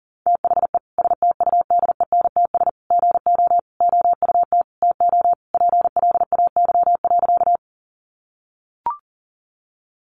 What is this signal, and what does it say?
Morse code, heard as a keyed tone, about 40 wpm, 700 Hz. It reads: THE STUDENTS GO OUT TO PLAY.
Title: Easy Phrases 40wpm